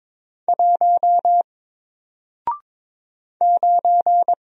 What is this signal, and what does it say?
Morse code, heard as a keyed tone, about 22 wpm, 700 Hz. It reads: 1 9
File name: Single-Letter-Number-22wpm